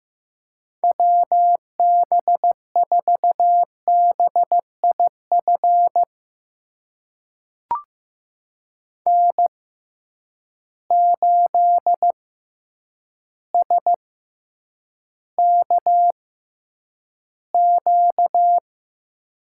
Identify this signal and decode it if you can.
Morse code, heard as a keyed tone, about 15 wpm, spaced as for 8 wpm, 700 Hz. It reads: WB4BIF N8SKQ
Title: USA Call Signs - 6x Character Spacing - 15wpm